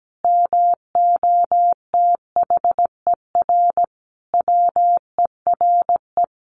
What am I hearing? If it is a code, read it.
Morse code, heard as a keyed tone, about 17 wpm, 700 Hz. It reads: MOTHER WERE
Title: Sentences from Top 1500 Words 17wpm